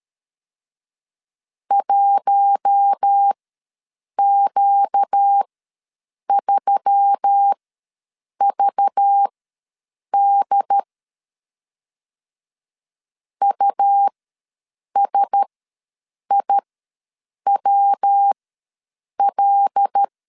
Morse code, heard as a keyed tone, about 13 wpm, spaced as for 7 wpm, 800 Hz. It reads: 1Q3VD USIWL
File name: LEZIONI / 6 PKX0 / 6 P ws 12wpm 3117